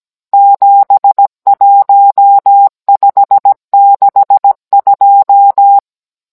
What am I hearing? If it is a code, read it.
Morse code, heard as a keyed tone, about 17 wpm, 800 Hz. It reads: 71562
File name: Cours-NR19-16mm